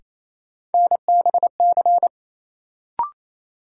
Morse code, heard as a keyed tone, about 28 wpm, 700 Hz. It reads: NBC
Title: Sets of 5 Words Encore - New Word Drill - 28wpm